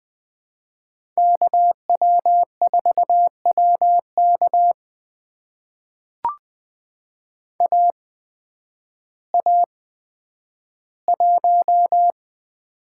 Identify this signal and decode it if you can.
Morse code, heard as a keyed tone, about 20 wpm, spaced as for 11 wpm, 700 Hz. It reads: KW4WK AA1
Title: USA Call Signs - 5 Characters - 8x Character Spacing - 20wpm